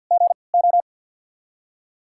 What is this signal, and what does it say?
Morse code, heard as a keyed tone, about 53 wpm, spaced as for 29 wpm, 690 Hz. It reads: GO
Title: 3x Character Spacing - Top 500 Words - 50wpm